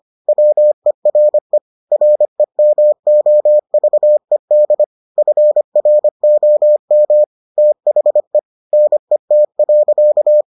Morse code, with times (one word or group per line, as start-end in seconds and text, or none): 0.3-1.6 WERE
1.9-4.8 REMOVED
5.2-7.2 FROM
7.6-8.4 THE
8.7-10.4 NET.